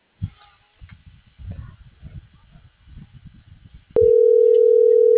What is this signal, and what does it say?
Ringback tone (North America), 440 + 480 Hz.